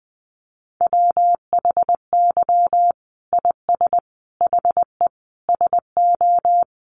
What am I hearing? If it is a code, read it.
Morse code, heard as a keyed tone, about 20 wpm, 700 Hz. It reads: WHY IS HE SO
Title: Sentences from Top 1200 Words 20wpm